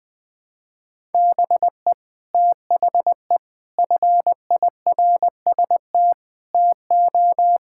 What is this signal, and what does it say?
Morse code, heard as a keyed tone, about 20 wpm, 700 Hz. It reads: BE THE FIRST TO
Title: Sets of 6 Words - 20wpm